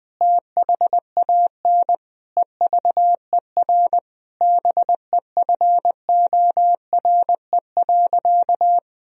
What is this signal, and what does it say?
Morse code, heard as a keyed tone, about 20 wpm, 700 Hz. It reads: THAN EVER BEFORE.
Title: Sentences from Top 1000 Words - Review Part 3 - 20wpm